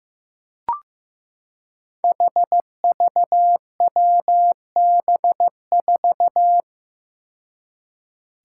Morse code, heard as a keyed tone, about 15 wpm, 700 Hz. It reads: HVWB4
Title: Five Challenging Letter-Number Combinations 15wpm